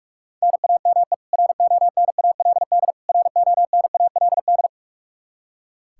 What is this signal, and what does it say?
Morse code, heard as a keyed tone, about 45 wpm, 700 Hz. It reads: NAME RONALD RONALD
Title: Practice QSO - Exchange 1 - Beginning - 45wpm